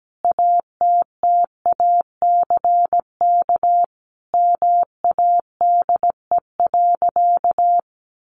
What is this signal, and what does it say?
Morse code, heard as a keyed tone, about 17 wpm, 700 Hz. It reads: ATTACK MADE.
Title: Sentences from Top 1500 Words 17wpm